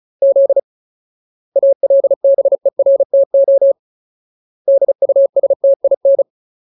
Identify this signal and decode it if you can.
Morse code, heard as a keyed tone, about 34 wpm, 550 Hz. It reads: Z ALBERTO DUSTIN
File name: Ultimate Word List / 35WPM Ultimate